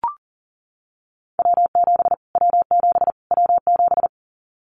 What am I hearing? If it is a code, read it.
Morse code, heard as a keyed tone, about 40 wpm, 700 Hz. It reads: W7 W7 W7
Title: Callsign Prefixes for Top 25 DXCC - Repeated 3x - 40wpm